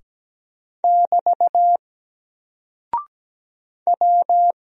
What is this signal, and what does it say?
Morse code, heard as a keyed tone, about 17 wpm, 700 Hz. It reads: <BT> W
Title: Single Character (Letter:Number:Punctuation:Prosign) Alt 17wpm